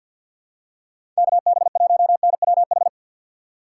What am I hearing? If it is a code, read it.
Morse code, heard as a keyed tone, about 48 wpm, 700 Hz. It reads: KB1NWH
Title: USA Call Signs - 3x Character Spacing - 50wpm